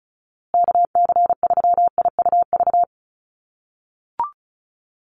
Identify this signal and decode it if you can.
Morse code, heard as a keyed tone, about 34 wpm, 700 Hz. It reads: KC3IUV
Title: USA Call Signs Repeated 2x 35wpm